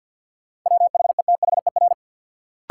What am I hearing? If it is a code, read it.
Morse code, heard as a keyed tone, about 53 wpm, 700 Hz. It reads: WHETHER
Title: Sentences from Top 1000 Words - Review Part 3 - New Word Drill - 50wpm